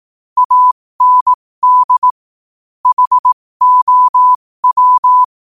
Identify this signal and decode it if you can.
Morse code, heard as a keyed tone, about 18 wpm, spaced as for 15 wpm, 1000 Hz. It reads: AND HOW
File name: qotdfiles / 15wpm